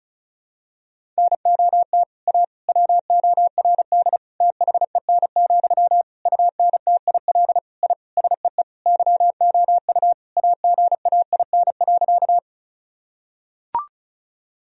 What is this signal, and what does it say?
Morse code, heard as a keyed tone, about 36 wpm, 700 Hz. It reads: NOT A WORD THEN, UNTIL I SEE YOU AGAIN.
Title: Sentences from Top 1000 Words - Review Part 1 - 35wpm